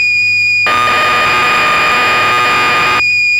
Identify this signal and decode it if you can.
Cassette data (Kansas City type).
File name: magazines / Acorn User / 1984-10 BBC Programs